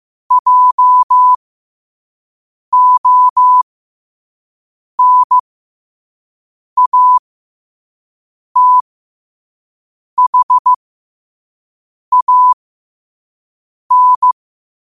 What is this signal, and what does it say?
Morse code, heard as a keyed tone, about 15 wpm, spaced as for 5 wpm, 1000 Hz. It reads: JONATHAN
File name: qotdfiles / 5wpm